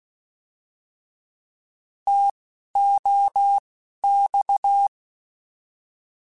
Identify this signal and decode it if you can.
Morse code, heard as a keyed tone, about 16 wpm, spaced as for 12 wpm, 800 Hz. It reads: TOX